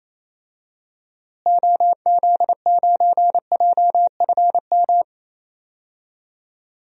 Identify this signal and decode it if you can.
Morse code, heard as a keyed tone, about 28 wpm, 700 Hz. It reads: OZ9JFM
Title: DXCC-100 Call Signs - 6 Characters - Repeated 2x - 28wpm